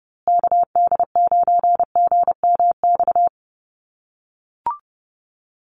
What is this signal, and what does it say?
Morse code, heard as a keyed tone, about 30 wpm, 700 Hz. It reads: KD9GMX
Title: USA Call Signs Repeated 3x 30wpm